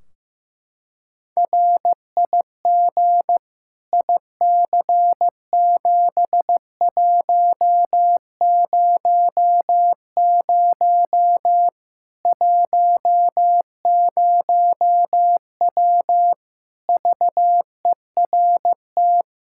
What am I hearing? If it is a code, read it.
Morse code, heard as a keyed tone, about 15 wpm, 700 Hz. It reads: RIG IC7100 10W VERT
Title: Practice QSO - Exchange 2 - Info - 15wpm